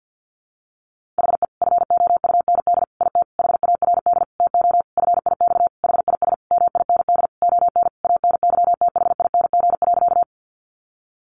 Morse code, heard as a keyed tone, about 50 wpm, 700 Hz. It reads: HE FOUND IT HARD TO FIX HIS MIND ON ANYTHING.
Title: Sentences from Top 1400 Words 50wpm